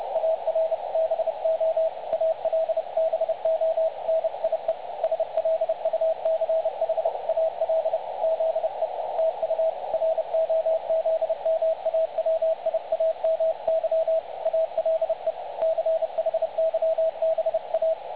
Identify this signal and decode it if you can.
Morse code, heard as a keyed tone, about 29 wpm, 660 Hz. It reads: ARBO ALBONIE SLUCHAL ZE TU ROZMAWIAMY ALE CHYBA
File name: dlazaawansowanych / nagraniazpasma / websdr recording 3565.0kHz